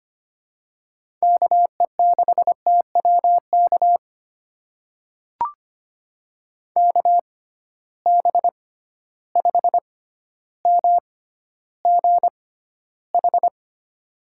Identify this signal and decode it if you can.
Morse code, heard as a keyed tone, about 25 wpm, spaced as for 13 wpm, 700 Hz. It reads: KE6TWK KB5MGH
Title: USA Call Signs - 6 Characters - 6x Character Spacing - 25wpm